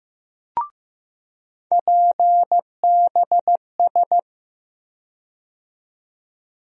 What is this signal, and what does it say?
Morse code, heard as a keyed tone, about 15 wpm, 700 Hz. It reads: PBS
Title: USA Call Sign Postfixes - 3 Letters 15wpm